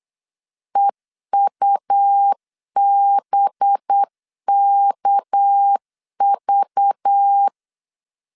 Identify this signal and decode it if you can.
Morse code, heard as a keyed tone, about 8 wpm, 800 Hz. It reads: EUBKV
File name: CW / DET / 1 6 1500C S B 8wpm 5142